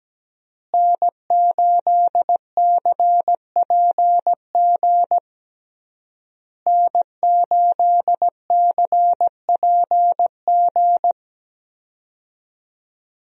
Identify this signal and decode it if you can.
Morse code, heard as a keyed tone, about 17 wpm, 700 Hz. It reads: N8CPG N8CPG
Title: USA Call Signs - 5 Characters - Repeated 3x - 17wpm